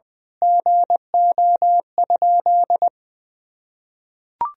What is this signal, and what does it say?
Morse code, heard as a keyed tone, about 21 wpm, 700 Hz. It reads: GO?
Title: Sentences from Top 1000 Words - Review Part 4 - 20wpm